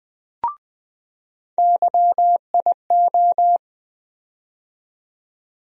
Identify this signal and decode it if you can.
Morse code, heard as a keyed tone, about 20 wpm, 700 Hz. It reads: YIO
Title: USA Call Sign Postfixes - 3 Letters 20wpm